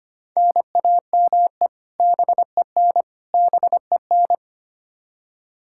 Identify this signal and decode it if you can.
Morse code, heard as a keyed tone, about 25 wpm, 700 Hz. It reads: NAME BEN BEN
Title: K1USN SST QSO Practice 25wpm